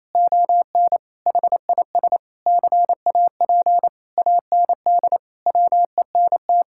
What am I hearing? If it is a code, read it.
Morse code, heard as a keyed tone, about 28 wpm, 700 Hz. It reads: ON HIS CAP AND WENT
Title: Sentences from Top 1600 Words 28wpm